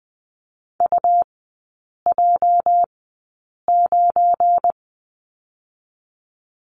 Morse code, heard as a keyed tone, about 20 wpm, spaced as for 8 wpm, 700 Hz. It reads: UJ9
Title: 205 Lesson 33e - Letter j - Three Characters 20wpm